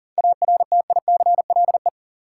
Morse code, heard as a keyed tone, about 40 wpm, 700 Hz. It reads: ARTICLE
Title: Sentences from Top 1000 Words - Review Part 5 - New Word Drill - 40wpm